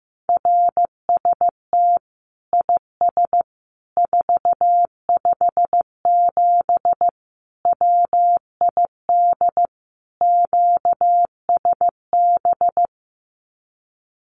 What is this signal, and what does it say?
Morse code, heard as a keyed tone, about 15 wpm, 700 Hz. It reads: RST IS 457 WID QSB
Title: Common QSO Phrases 15wpm